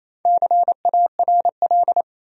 Morse code, heard as a keyed tone, about 28 wpm, 700 Hz. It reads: CARL
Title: USA Common Names - Max 5 Letters - 28wpm